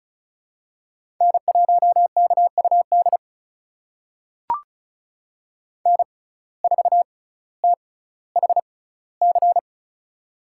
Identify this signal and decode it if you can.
Morse code, heard as a keyed tone, about 34 wpm, spaced as for 18 wpm, 700 Hz. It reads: N1KUD N4THC
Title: USA Call Signs - 5 Characters - 6x Character Spacing - 35wpm